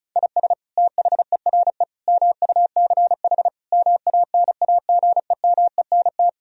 Morse code, heard as a keyed tone, about 36 wpm, 700 Hz. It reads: IS THERE MUCH MANAGEMENT
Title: Sentences from Top 1000 Words - Review Part 4 - 35wpm